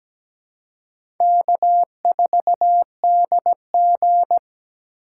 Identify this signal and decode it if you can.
Morse code, heard as a keyed tone, about 17 wpm, 700 Hz. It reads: K4DG
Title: USA Call Signs - 4x Character Spacing - 17wpm